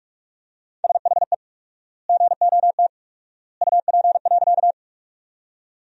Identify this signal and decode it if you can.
Morse code, heard as a keyed tone, about 42 wpm, 700 Hz. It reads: SHE GOT UP.
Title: 4x Word-Spacing - Sentences from Top 300 Words - 45wpm